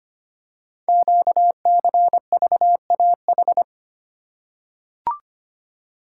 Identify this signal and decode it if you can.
Morse code, heard as a keyed tone, about 25 wpm, 700 Hz. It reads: QCVAH
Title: Code Groups - 5 Letters - 25wpm